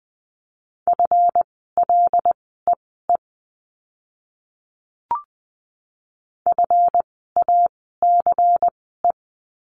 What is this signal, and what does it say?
Morse code, heard as a keyed tone, about 20 wpm, spaced as for 15 wpm, 700 Hz. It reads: FLEE FACE
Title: 139 Lesson 23f - Letter f - Words 20wpm